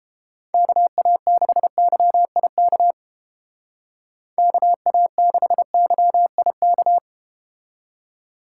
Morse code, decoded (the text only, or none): KA6YIK KA6YIK